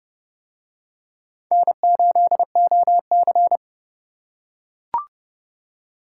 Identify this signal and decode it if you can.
Morse code, heard as a keyed tone, about 27 wpm, 700 Hz. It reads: N8OC